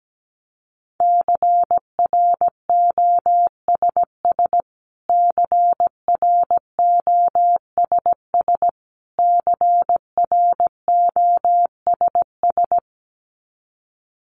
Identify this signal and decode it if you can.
Morse code, heard as a keyed tone, about 17 wpm, 700 Hz. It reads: CROSS CROSS CROSS
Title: Top 300 Words Repeated 17wpm